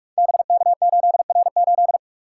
Morse code, heard as a keyed tone, about 44 wpm, 700 Hz. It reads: BK8R8
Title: Code Groups - 5 Letter:Numbers - 45wpm